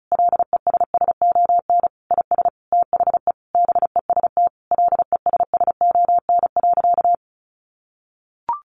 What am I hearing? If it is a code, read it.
Morse code, heard as a keyed tone, about 34 wpm, 700 Hz. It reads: LESSON IS THE BEST LESSON.
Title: Sentences from Top 1400 Words 35wpm